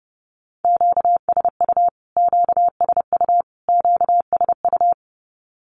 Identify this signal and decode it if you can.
Morse code, heard as a keyed tone, about 30 wpm, 700 Hz. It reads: QSU QSU QSU
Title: Q-Codes Repeated 3x 30wpm